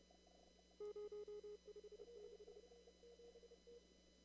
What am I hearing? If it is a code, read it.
Morse code, heard as a keyed tone, about 30 wpm, 420 Hz. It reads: 056
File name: CW / kt5j wpx cw-op-k5pi